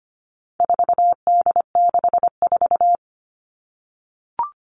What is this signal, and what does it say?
Morse code, heard as a keyed tone, about 25 wpm, 700 Hz. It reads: 4D64
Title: All Cut Numbers - 4-Digit 25wpm